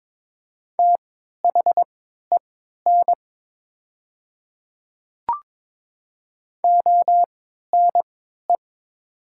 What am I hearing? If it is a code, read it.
Morse code, heard as a keyed tone, about 23 wpm, spaced as for 13 wpm, 700 Hz. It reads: THEN ONE
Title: 3x Character Spacing - Top 100 Words - 22wpm